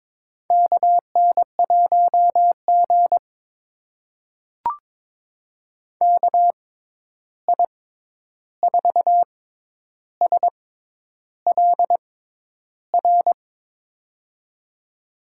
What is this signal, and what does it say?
Morse code, heard as a keyed tone, about 23 wpm, spaced as for 10 wpm, 700 Hz. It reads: KN1G KI4SLR
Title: USA Call Signs - 6x Character Spacing - 22wpm